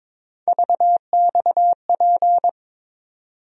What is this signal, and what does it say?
Morse code, heard as a keyed tone, about 22 wpm, 700 Hz. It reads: VXP
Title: USA Call Sign Postfixes - 3 Letters 22wpm